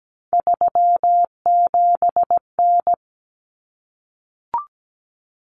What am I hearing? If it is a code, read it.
Morse code, heard as a keyed tone, about 17 wpm, 700 Hz. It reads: 37N
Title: Common Cut Numbers - 3-Digit 17wpm